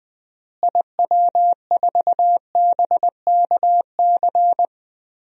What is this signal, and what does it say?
Morse code, heard as a keyed tone, about 20 wpm, 700 Hz. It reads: IW4BKC
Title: DXCC-100 Call Signs - 6 Characters - Repeated 2x - 20wpm